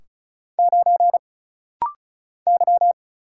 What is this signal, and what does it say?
Morse code, heard as a keyed tone, about 36 wpm, 700 Hz. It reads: 9 Y
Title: Single Letter-Number - Rapid-Fire 35wpm